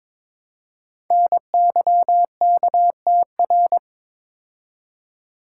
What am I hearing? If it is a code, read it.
Morse code, heard as a keyed tone, about 22 wpm, 700 Hz. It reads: NYKTR
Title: Code Groups - 5 Letters - 22wpm